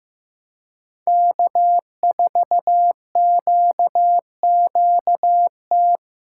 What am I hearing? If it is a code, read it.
Morse code, heard as a keyed tone, about 15 wpm, 700 Hz. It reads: K4QQT
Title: USA Call Signs - 5 Characters - 2x Character Spacing - 15wpm